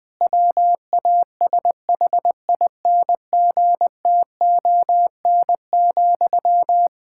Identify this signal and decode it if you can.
Morse code, heard as a keyed tone, about 20 wpm, 700 Hz. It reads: WASHINGTON,
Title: Practice QSO - Exchange 1 - Beginning - 20wpm